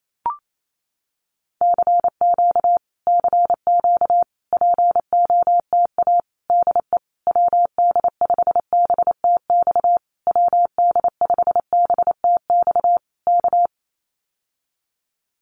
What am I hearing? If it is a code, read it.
Morse code, heard as a keyed tone, about 28 wpm, 700 Hz. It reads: CQ CQ POTA DE WD5BTX WD5BTX K
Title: Common QSO Phrases Encore 28wpm